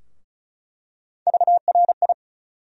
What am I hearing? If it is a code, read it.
Morse code, heard as a keyed tone, about 34 wpm, 700 Hz. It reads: VRI
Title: USA Call Sign Postfixes - 3 Letters 35wpm